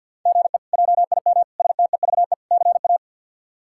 Morse code, heard as a keyed tone, about 46 wpm, 690 Hz. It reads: GE JIM STEVE CA
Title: K1USN SST QSO Practice - 50wpm